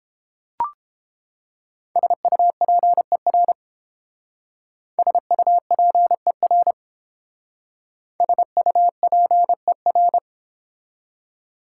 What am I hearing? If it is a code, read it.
Morse code, heard as a keyed tone, about 28 wpm, 700 Hz. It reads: SUPER SUPER SUPER